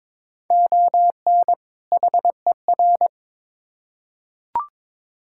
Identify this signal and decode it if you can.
Morse code, heard as a keyed tone, about 22 wpm, 700 Hz. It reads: ON HER
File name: Build A Phrase - 2-Word Phrases - 22wpm